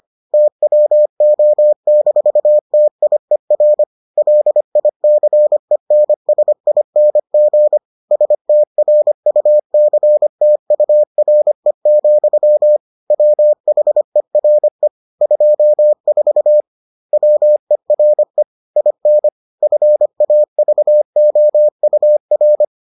Morse code, heard as a keyed tone, about 25 wpm, 600 Hz. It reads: TWO-TIER LICENSING STRUCTURE, WHERE 24 WERE IN FAVOUR